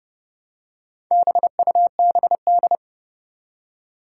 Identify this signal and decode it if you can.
Morse code, heard as a keyed tone, about 31 wpm, 700 Hz. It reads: BUBD
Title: Four Challenging Letter-Number Combinations 30wpm